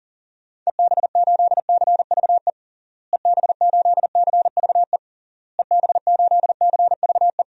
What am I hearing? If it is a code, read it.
Morse code, heard as a keyed tone, about 40 wpm, 700 Hz. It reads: EB8CVE EB8CVE EB8CVE
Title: DXCC-100 Call Signs - 6 Characters - Repeated 3x - 40wpm